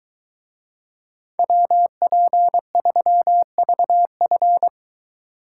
Morse code, heard as a keyed tone, about 23 wpm, 700 Hz. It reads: WP3VF